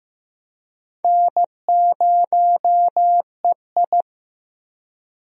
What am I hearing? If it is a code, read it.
Morse code, heard as a keyed tone, about 15 wpm, 700 Hz. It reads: N0EI
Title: USA Call Signs - 4 Characters - 15wpm